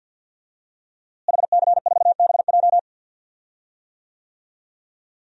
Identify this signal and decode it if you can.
Morse code, heard as a keyed tone, about 47 wpm, 700 Hz. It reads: HC4BJ
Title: DXCC-100 Call Signs - 5 Characters - Repeated 2x - 50wpm